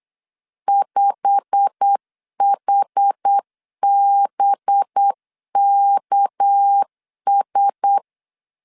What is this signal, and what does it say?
Morse code, heard as a keyed tone, about 8 wpm, 800 Hz. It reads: 5HBKS
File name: LEZIONI / 8 YJ45 / 8 YJ45 8wpm 3214